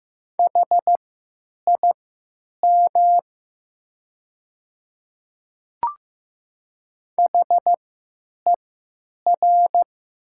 Morse code, heard as a keyed tone, about 15 wpm, spaced as for 9 wpm, 700 Hz. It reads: HIM HER
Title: 3x Character Spacing - Top 100 Words - 15wpm